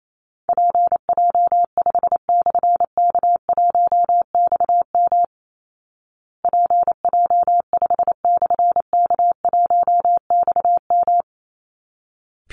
Morse code, heard as a keyed tone, about 28 wpm, 700 Hz. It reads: PJ5/K1XM PJ5/K1XM
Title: Call Signs Repeated 3x - 28wpm